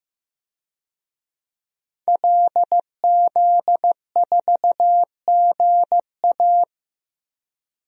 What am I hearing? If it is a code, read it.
Morse code, heard as a keyed tone, about 15 wpm, 700 Hz. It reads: LZ4GA
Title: DXCC-100 Call Signs - 5 Characters - 15wpm